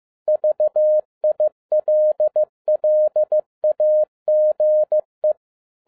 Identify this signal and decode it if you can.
Morse code, heard as a keyed tone, about 15 wpm, 610 Hz. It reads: VILLAGE
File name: 2011-10-mp3 / n02-FIELD DAY CONTEST QRV FROM SAKAE VILLAGE 15wpm